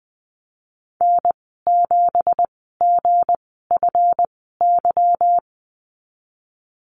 Morse code, heard as a keyed tone, about 20 wpm, spaced as for 15 wpm, 700 Hz. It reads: N7GFY
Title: 179 Lesson 29g - Slash - Call Signs 20wpm